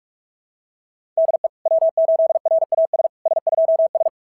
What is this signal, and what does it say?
Morse code, heard as a keyed tone, about 44 wpm, 690 Hz. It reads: DE W8RAS S2S
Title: SOTA QSO Practice 45wpm